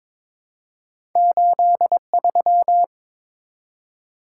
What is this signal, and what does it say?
Morse code, heard as a keyed tone, about 22 wpm, 700 Hz. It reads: 83
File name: practice / 22wpm / Two Letter-Number Combinations - 22wpm